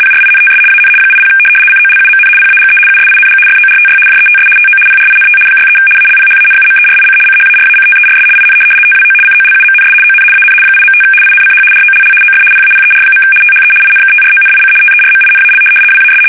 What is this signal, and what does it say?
RTTY, about 75 baud. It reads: PRINTS AN 80 COLUMN PAGE USING FORTY JETS TO PRINT 63 ALPHANUMERIC CHARACTERS IN TWO COLUMNS ON AN 8.5 INCH ROLL OF PAPER. THE INKTRONIC TERMINAL WAS
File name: fsk850-75B / teletype-corp